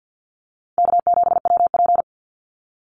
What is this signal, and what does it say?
Morse code, heard as a keyed tone, about 49 wpm, 700 Hz. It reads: K7WP
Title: Easy Call Signs Repeated 2x - 50wpm